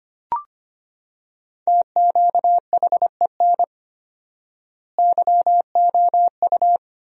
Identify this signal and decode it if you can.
Morse code, heard as a keyed tone, about 25 wpm, 700 Hz. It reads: TQHEN YOU
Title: 4x Word-Spacing - Sentences from Top 100 Words - 25wpm